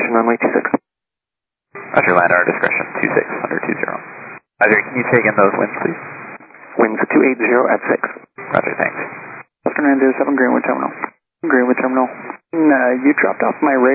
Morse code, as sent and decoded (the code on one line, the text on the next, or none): none